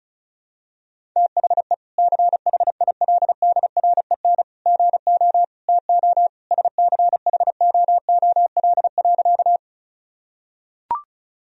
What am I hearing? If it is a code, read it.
Morse code, heard as a keyed tone, about 36 wpm, 700 Hz. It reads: THE CHILDREN GO TO SCHOOL.
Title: Sentences from Top 1000 Words - Review Part 1 - 35wpm